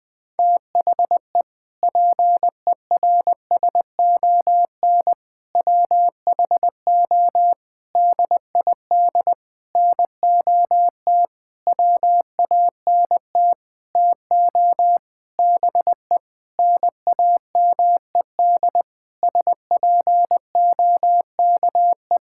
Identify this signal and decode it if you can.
Morse code, heard as a keyed tone, about 20 wpm, 700 Hz. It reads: THE PERSON WHO DID NOT WANT TO BE NAMED SPOKE
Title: Sentences with 7-Word Phrases - 20wpm